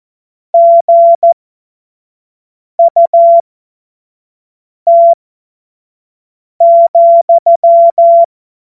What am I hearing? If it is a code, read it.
Morse code, heard as a keyed tone, about 14 wpm, spaced as for 5 wpm, 680 Hz. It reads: GUT,
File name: Morsekurs / Witze / Witz 20. Dezember 2022 25 F70